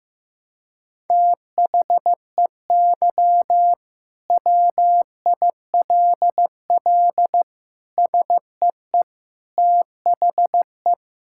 Morse code, heard as a keyed tone, about 15 wpm, 700 Hz. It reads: THEY WILL SEE THE
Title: Sentences from Top 1600 Words 15wpm